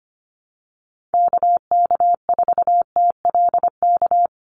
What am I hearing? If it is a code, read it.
Morse code, heard as a keyed tone, about 25 wpm, 700 Hz. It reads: KK4TLK
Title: USA Call Signs 25wpm